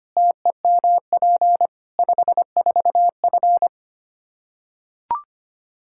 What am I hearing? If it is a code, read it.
Morse code, heard as a keyed tone, about 25 wpm, 700 Hz. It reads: TEMP 54F
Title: Practice QSO - Exchange 2 - Info - 25wpm